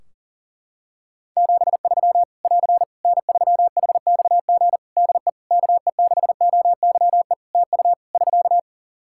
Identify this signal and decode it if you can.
Morse code, heard as a keyed tone, about 40 wpm, 700 Hz. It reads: 73 <AR> N3HXG DE KE6OYE TU <SK>
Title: Practice QSO - Exchange 4 - Ending - 40wpm